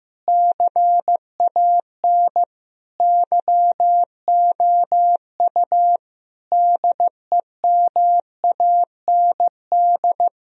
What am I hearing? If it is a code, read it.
Morse code, heard as a keyed tone, about 15 wpm, 700 Hz. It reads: CAN YOU DEMAND
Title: Sentences from Top 1000 Words - Review Part 4 - 15wpm